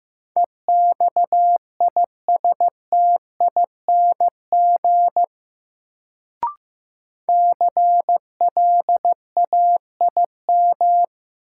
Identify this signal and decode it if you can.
Morse code, heard as a keyed tone, about 16 wpm, 700 Hz. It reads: EXISTING CLAIM
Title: Sentences from Top 1500 Words - New Word Drill Alt - 15wpm